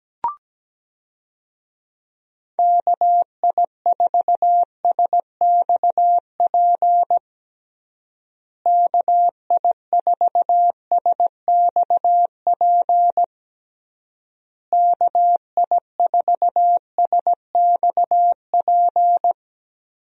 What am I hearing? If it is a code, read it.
Morse code, heard as a keyed tone, about 17 wpm, 700 Hz. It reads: KI4SXP KI4SXP KI4SXP
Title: USA Call Signs - 6 Characters - Repeated 3x - 17wpm